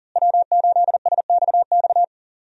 Morse code, heard as a keyed tone, about 40 wpm, 700 Hz. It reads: W8SXX
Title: USA Call Signs - 5 Characters - Repeated 3x - 40wpm